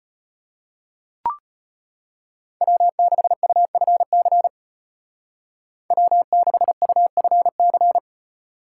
Morse code, heard as a keyed tone, about 33 wpm, 700 Hz. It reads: W6UFC W6UFC